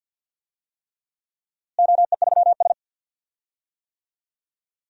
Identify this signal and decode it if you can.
Morse code, heard as a keyed tone, about 51 wpm, 700 Hz. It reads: OE3S
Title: DXCC-100 Call Signs - 4 Characters - Repeated 3x - 50wpm